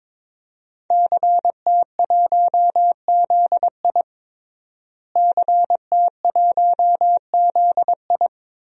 Morse code, heard as a keyed tone, about 22 wpm, 700 Hz. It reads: CT1ZI CT1ZI
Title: DXCC-100 Call Signs - 5 Characters - Repeated 2x - 22wpm